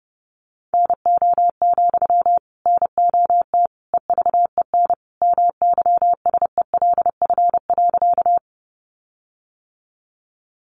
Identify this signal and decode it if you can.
Morse code, heard as a keyed tone, about 30 wpm, 700 Hz. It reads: NO, NOT EVEN MYSELF.
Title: Sentences from Top 1300 Words 30wpm